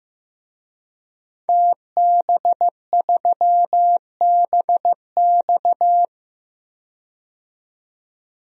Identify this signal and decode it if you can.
Morse code, heard as a keyed tone, about 15 wpm, 700 Hz. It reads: TB3BX
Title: DXCC-100 Call Signs - 5 Characters - 15wpm